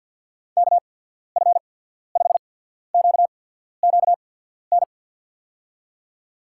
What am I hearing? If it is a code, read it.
Morse code, heard as a keyed tone, about 53 wpm, spaced as for 14 wpm, 700 Hz. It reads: KF5QQN
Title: USA Call Signs - 6 Characters - 8x Character Spacing - 50wpm